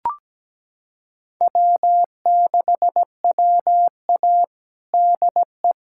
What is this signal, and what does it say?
Morse code, heard as a keyed tone, about 17 wpm, 700 Hz. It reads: W6WA DE
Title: USA Callsign DE Callsign - 4 Characters - 17wpm